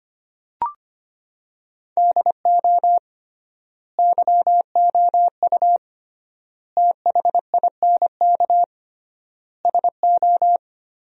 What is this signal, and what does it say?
Morse code, heard as a keyed tone, about 25 wpm, 700 Hz. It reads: DO YOU THINK SO
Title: 3x Word-Spacing - Sentences from Top 200 Words - 25wpm